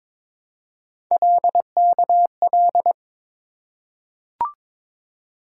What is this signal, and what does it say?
Morse code, heard as a keyed tone, about 22 wpm, 700 Hz. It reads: LKL
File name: practice / 22wpm / Three Letter-Number Combinations - 22wpm